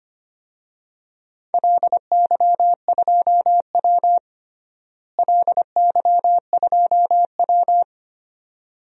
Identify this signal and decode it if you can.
Morse code, heard as a keyed tone, about 25 wpm, 700 Hz. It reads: LY2W LY2W
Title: DXCC-100 Call Signs - 4 Characters - Repeated 2x - 25wpm